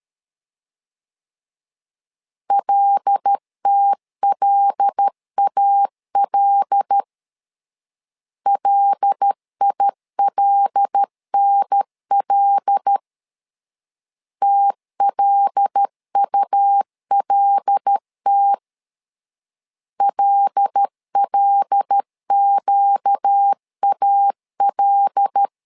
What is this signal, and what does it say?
Morse code, heard as a keyed tone, about 13 wpm, 800 Hz. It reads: LTLAL LILNL TLULT LLQAL
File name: LEZIONI / 5 LBVC / 5 L 12wpm 1913